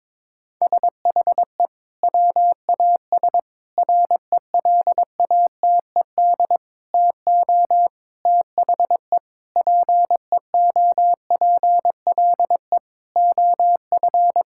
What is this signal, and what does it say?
Morse code, heard as a keyed tone, about 22 wpm, 700 Hz. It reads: SHE WAS RELATED TO THE PEOPLE OF
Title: Sentences from Top 1000 Words - Review Part 5 - 22wpm